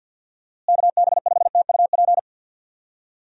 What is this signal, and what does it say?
Morse code, heard as a keyed tone, about 47 wpm, 700 Hz. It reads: KB5TUP
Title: USA Call Signs - 6 Characters - Repeated 2x - 50wpm